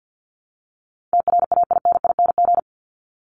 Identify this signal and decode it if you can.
Morse code, heard as a keyed tone, about 50 wpm, 700 Hz. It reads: TRAINING
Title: Top 501-750 Words 50wpm